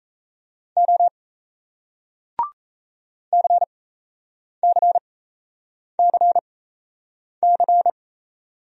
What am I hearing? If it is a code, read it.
Morse code, heard as a keyed tone, about 38 wpm, spaced as for 8 wpm, 700 Hz. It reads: O CCCC